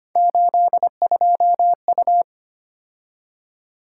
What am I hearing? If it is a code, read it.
Morse code, heard as a keyed tone, about 25 wpm, 700 Hz. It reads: 82U
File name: Three Letter-Number Combinations - 25wpm